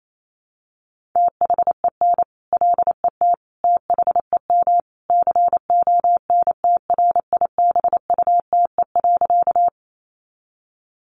Morse code, heard as a keyed tone, about 28 wpm, 700 Hz. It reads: THEN LET THEM CONTRIBUTE.
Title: Sentences from Top 1400 Words 28wpm